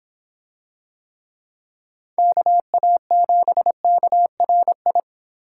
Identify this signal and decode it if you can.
Morse code, heard as a keyed tone, about 24 wpm, 700 Hz. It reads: KA7KRI